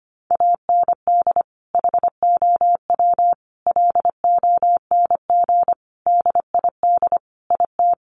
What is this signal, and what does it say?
Morse code, heard as a keyed tone, about 25 wpm, 700 Hz. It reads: AND HOW LONG DID IT
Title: Sentences from Top 1500 Words 25wpm